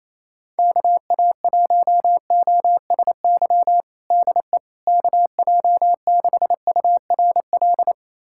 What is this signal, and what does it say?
Morse code, heard as a keyed tone, about 28 wpm, 700 Hz. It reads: KA1OSY DE KJ6URL
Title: USA Callsign DE Callsign - 6 Characters - 28wpm